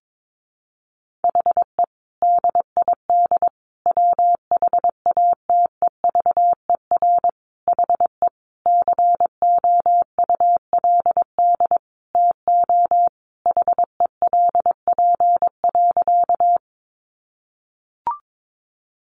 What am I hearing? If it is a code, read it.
Morse code, heard as a keyed tone, about 22 wpm, 700 Hz. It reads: HE DID WHATEVER HE COULD TO HELP.
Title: Sentences from Top 1100 Words 22wpm